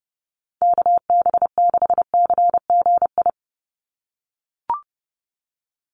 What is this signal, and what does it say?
Morse code, heard as a keyed tone, about 30 wpm, 700 Hz. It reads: KB6CGI
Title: USA Call Signs 30wpm